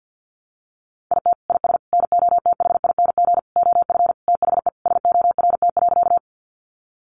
Morse code, heard as a keyed tone, about 48 wpm, 700 Hz. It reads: IT IS NOTHING OF THE SORT.
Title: Sentences from Top 1300 Words 50wpm